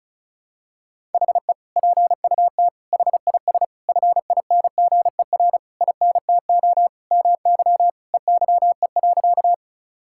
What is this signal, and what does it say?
Morse code, heard as a keyed tone, about 36 wpm, 700 Hz. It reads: HE PUT HIS FINGER INTO MY EYE.
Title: Sentences from Top 1600 Words 35wpm